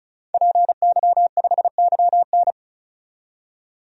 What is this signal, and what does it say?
Morse code, heard as a keyed tone, about 34 wpm, 700 Hz. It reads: PY5YN
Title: DXCC-100 Call Signs - 5 Characters - 35wpm